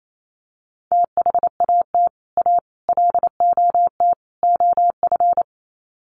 Morse code, heard as a keyed tone, about 28 wpm, 700 Hz. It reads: THAT A LOT OF
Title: Sets of 4 Words 28wpm